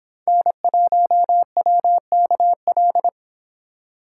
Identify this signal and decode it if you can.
Morse code, heard as a keyed tone, about 26 wpm, 700 Hz. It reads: N1WKL